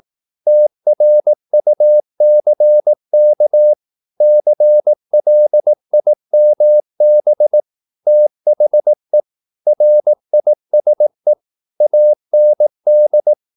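Morse code, heard as a keyed tone, about 18 wpm, 600 Hz. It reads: TRUCK CLIMB THE RISE AND
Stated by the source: Morse practice at 18 wpm, English text, from CW med Gnister.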